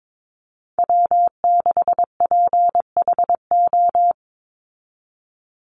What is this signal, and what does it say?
Morse code, heard as a keyed tone, about 22 wpm, 700 Hz. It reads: W6PHO
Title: Call Signs Repeated 2x - 22wpm